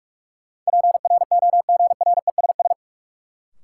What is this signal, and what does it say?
Morse code, heard as a keyed tone, about 46 wpm, 700 Hz. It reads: PROGRESS
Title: Sentences from Top 1400 Words - New Word Drill Alt - 45wpm